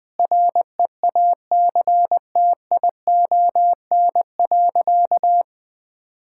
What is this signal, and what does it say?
Morse code, heard as a keyed tone, about 20 wpm, 700 Hz. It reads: REACTION.
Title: Sentences with 8-Word Phrases - 20wpm